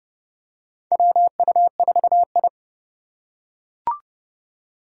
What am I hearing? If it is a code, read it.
Morse code, heard as a keyed tone, about 30 wpm, 700 Hz. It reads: WU4I
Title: USA Call Signs - 2x Character Spacing - 30wpm